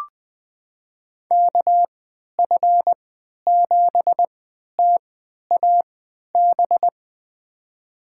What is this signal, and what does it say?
Morse code, heard as a keyed tone, about 20 wpm, spaced as for 11 wpm, 700 Hz. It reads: KF7TAB
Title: USA Call Signs - 6 Characters - 3x Character Spacing - 20wpm